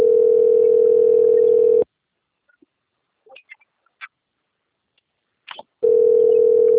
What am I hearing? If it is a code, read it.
Ringback tone (North America), 440 + 480 Hz.
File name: rg-1600-unknown- - - .4451